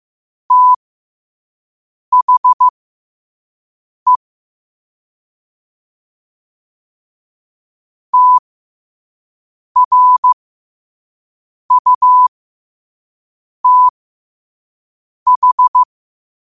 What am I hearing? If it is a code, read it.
Morse code, heard as a keyed tone, about 15 wpm, spaced as for 5 wpm, 1000 Hz. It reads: THE TRUTH
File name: qotdfiles / 5wpm